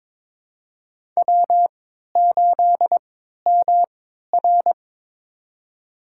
Morse code, heard as a keyed tone, about 22 wpm, spaced as for 12 wpm, 700 Hz. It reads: W8MR
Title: USA Call Signs - 4 Characters - 3x Character Spacing - 22wpm